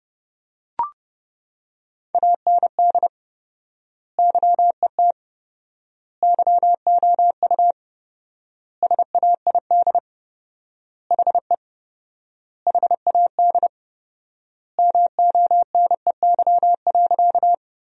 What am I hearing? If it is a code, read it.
Morse code, heard as a keyed tone, about 30 wpm, 700 Hz. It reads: AND YET YOU SAID HE HAD MONEY.
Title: 4x Word-Spacing - Sentences from Top 500 Words - 30wpm